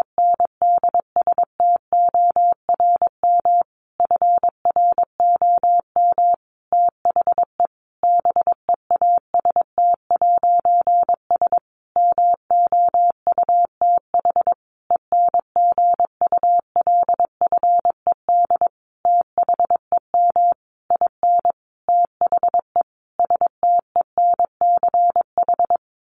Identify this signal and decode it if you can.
Morse code, heard as a keyed tone, about 22 wpm, 700 Hz. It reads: NDSTORM FROM THE BEAST'S MOUTH ENGULFED THEM IN THE STENCH
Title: A Sound Of Thunder 22wpm